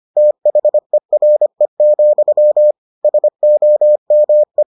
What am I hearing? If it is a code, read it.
Morse code, heard as a keyed tone, about 25 wpm, 600 Hz. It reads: THERE, SOME